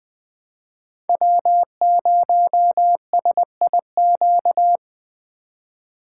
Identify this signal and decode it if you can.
Morse code, heard as a keyed tone, about 20 wpm, 700 Hz. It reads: W0SIQ
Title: Speed Racing - USA 1x3 Call Signs 17wpm